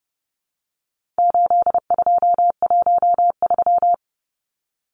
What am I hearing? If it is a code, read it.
Morse code, heard as a keyed tone, about 30 wpm, 700 Hz. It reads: 8213
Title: Numbers - 4-Digit 30wpm